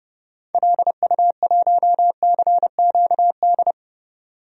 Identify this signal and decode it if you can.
Morse code, heard as a keyed tone, about 31 wpm, 700 Hz. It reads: LU1CQD
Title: DXCC-100 Call Signs - 6 Characters - 30wpm